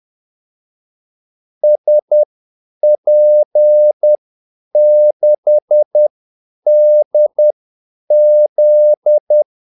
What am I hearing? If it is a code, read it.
Morse code, heard as a keyed tone, about 10 wpm, spaced as for 8 wpm, 600 Hz. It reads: SP6DZ
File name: Callsigns-10-8wpm